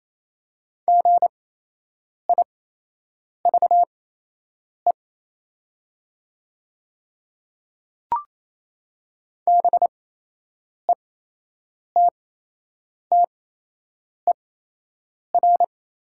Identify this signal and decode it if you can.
Morse code, heard as a keyed tone, about 28 wpm, spaced as for 8 wpm, 700 Hz. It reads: GIVE BETTER
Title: 8x Character Spacing - Top 500 Words - 28wpm